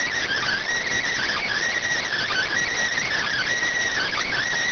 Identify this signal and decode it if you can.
Emergency Alert System (SAME) tones (header bursts), heard as harsh buzzing bursts. It is heard from 0.2 to 4.4 s.